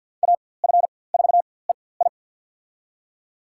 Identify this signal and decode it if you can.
Morse code, heard as a keyed tone, about 47 wpm, spaced as for 23 wpm, 700 Hz. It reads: AF4EI
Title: USA Call Signs - 5 Characters - 4x Character Spacing - 50wpm